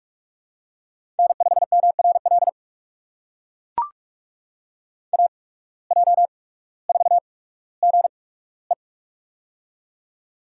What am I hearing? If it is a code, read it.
Morse code, heard as a keyed tone, about 44 wpm, spaced as for 19 wpm, 700 Hz. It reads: N5MRL AJ4GE
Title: USA Call Signs - 5 Characters - 8x Character Spacing - 45wpm